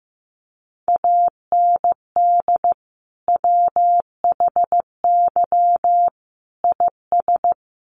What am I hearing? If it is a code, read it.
Morse code, heard as a keyed tone, about 15 wpm, 700 Hz. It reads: AND WHY IS
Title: Sentences from Top 200 Words 15wpm